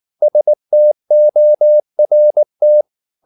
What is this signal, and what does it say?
Morse code, heard as a keyed tone, about 19 wpm, 600 Hz. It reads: STORT